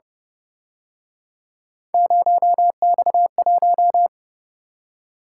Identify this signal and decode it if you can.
Morse code, heard as a keyed tone, about 30 wpm, 700 Hz. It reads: 0X1
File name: Speed-Racing - Three Letter-Number Combinations - 25wpm